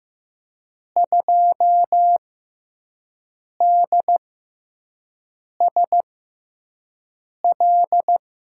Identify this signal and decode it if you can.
Morse code, heard as a keyed tone, about 15 wpm, spaced as for 5 wpm, 700 Hz. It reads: 2DSL
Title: USA Call Signs - 5 Characters - 6x Character Spacing - 15wpm